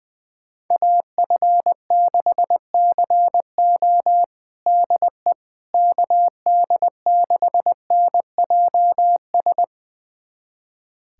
Morse code, heard as a keyed tone, about 19 wpm, 700 Hz. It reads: AF6CO DE KD6NJS
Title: USA Callsign DE Callsign 20wpm